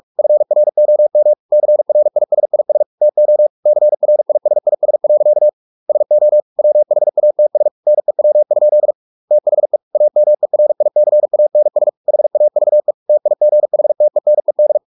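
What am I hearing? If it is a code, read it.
Morse code, heard as a keyed tone, about 44 wpm, 600 Hz. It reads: FROM CRISIS TO CRISIS. SO WHATS NEW? THE AMERICANS HAVE TIGHTENED
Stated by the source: Morse practice at 45 wpm, the novel Never by Ken Follett in English, from CW med Gnister.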